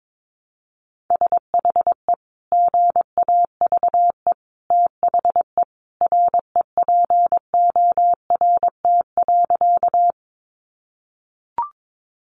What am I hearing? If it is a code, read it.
Morse code, heard as a keyed tone, about 22 wpm, 700 Hz. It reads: SHE GAVE THE REPORT.
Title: Sentences from Top 900 Words 22wpm